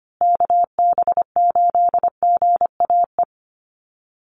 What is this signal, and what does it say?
Morse code, heard as a keyed tone, about 25 wpm, 700 Hz. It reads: KB8GAE
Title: Call Signs Repeated 2x - 25wpm